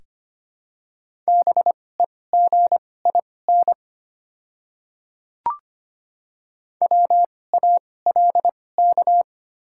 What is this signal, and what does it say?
Morse code, heard as a keyed tone, about 25 wpm, spaced as for 18 wpm, 700 Hz. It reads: BEGIN WALK
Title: 2x Character Spacing - Top 300 Words - 25wpm